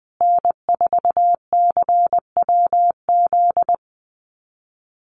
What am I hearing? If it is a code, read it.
Morse code, heard as a keyed tone, about 20 wpm, 700 Hz. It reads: N4CWZ
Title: USA Call Signs Repeated 3x 20wpm